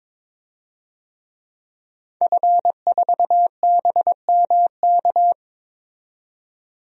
Morse code, heard as a keyed tone, about 22 wpm, 700 Hz. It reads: F4BMK
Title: DXCC-100 Call Signs - 5 Characters - 22wpm